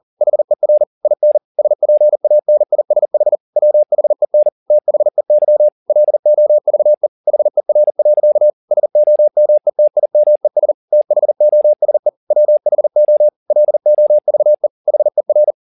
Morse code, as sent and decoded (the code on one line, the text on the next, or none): .... .-. / .. -. / ... .--. .- -. .. ... .... / .-- .... -. / - .... -.-- / .-.. --- ...- . / .... .-. .-.-.- / ... --- -- . - .. -- . ... / - .... --- ... / .-- .... --- / .-.. --- ...- . / .... .-.
HER IN SPANISH WHEN THEY LOVE HER. SOMETIMES THOSE WHO LOVE HER